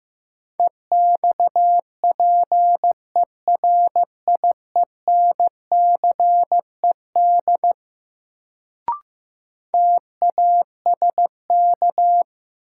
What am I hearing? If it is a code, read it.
Morse code, heard as a keyed tone, about 15 wpm, 700 Hz. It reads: EXPERIENCED TASK
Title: Sentences from Top 1100 Words - New Word Drill Alt - 15wpm